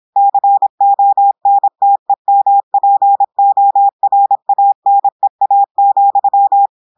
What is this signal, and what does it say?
Morse code, heard as a keyed tone, about 26 wpm, 800 Hz. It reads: CONTEMPORANEA,
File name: audios / portugais / portugais texte en clair 1 vitesse 26 mots minutes